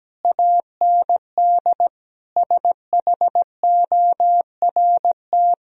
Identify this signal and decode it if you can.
Morse code, heard as a keyed tone, about 17 wpm, 700 Hz. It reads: AND SHORT
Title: Sentences from Top 1000 Words - Review Part 2 - 17wpm